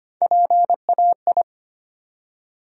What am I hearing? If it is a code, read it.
Morse code, heard as a keyed tone, about 25 wpm, 700 Hz. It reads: PAI
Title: USA Call Sign Postfixes - 3 Letters 25wpm